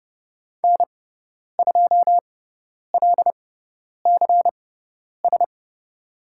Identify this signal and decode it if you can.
Morse code, heard as a keyed tone, about 30 wpm, spaced as for 10 wpm, 700 Hz. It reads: N2LCS
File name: Lesson 16g - Letter c - Call Signs 30wpm